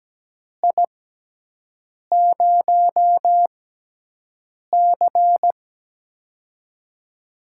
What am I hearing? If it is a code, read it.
Morse code, heard as a keyed tone, about 17 wpm, spaced as for 6 wpm, 700 Hz. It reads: I0C
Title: USA Call Signs - 4 Characters - 6x Character Spacing - 17wpm